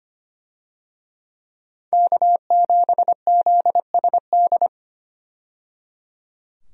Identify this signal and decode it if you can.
Morse code, heard as a keyed tone, about 25 wpm, 700 Hz. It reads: K7ZSD
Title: USA Call Signs - 5 Characters - 25wpm